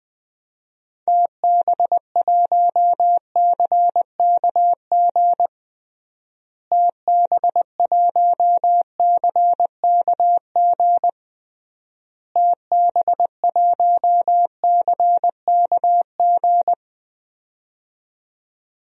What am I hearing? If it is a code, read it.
Morse code, heard as a keyed tone, about 21 wpm, 700 Hz. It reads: TB1CKG TB1CKG TB1CKG
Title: DXCC-100 Call Signs - 6 Characters - Repeated 3x - 20wpm